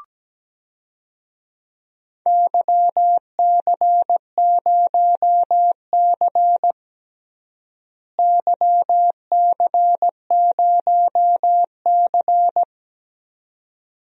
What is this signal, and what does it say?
Morse code, heard as a keyed tone, about 17 wpm, 700 Hz. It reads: YC0C YC0C
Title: DXCC-100 Call Signs - 4 Characters - Repeated 2x - 17wpm